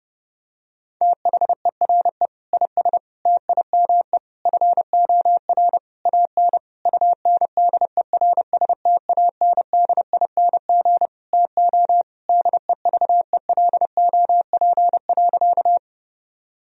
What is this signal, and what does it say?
Morse code, heard as a keyed tone, about 30 wpm, 700 Hz. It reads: THERE IS TIME FOR AN UNDERSTANDING TO DEVELOP.
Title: Sentences from Top 1000 Words - Review Part 3 - 30wpm